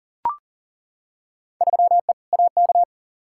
Morse code, heard as a keyed tone, about 40 wpm, 700 Hz. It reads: 3E AK
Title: Field Day Exchanges Encore - 40wpm